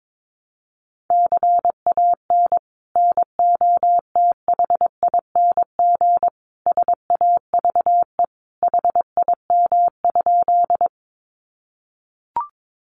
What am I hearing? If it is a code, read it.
Morse code, heard as a keyed tone, about 22 wpm, 700 Hz. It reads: CAN NOTHING SAVE HIM?
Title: Sentences from Top 1100 Words 22wpm